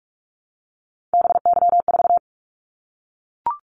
Morse code, heard as a keyed tone, about 44 wpm, 700 Hz. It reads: BY4
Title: Callsign Prefixes for Top 100 DXCC - 45wpm